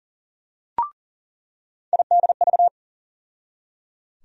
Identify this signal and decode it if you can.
Morse code, heard as a keyed tone, about 37 wpm, 700 Hz. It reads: IDV
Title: USA Call Sign Postfixes - 3 Letters 40wpm